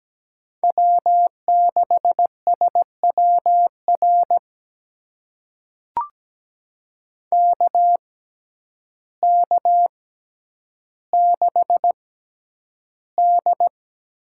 Morse code, heard as a keyed tone, about 17 wpm, spaced as for 9 wpm, 700 Hz. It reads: W6SWR KK6D
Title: USA Call Signs - 5 Characters - 6x Character Spacing - 17wpm